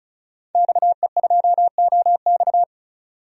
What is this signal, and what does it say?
Morse code, heard as a keyed tone, about 36 wpm, 700 Hz. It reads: XE2OX
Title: DXCC-100 Call Signs - 5 Characters - 35wpm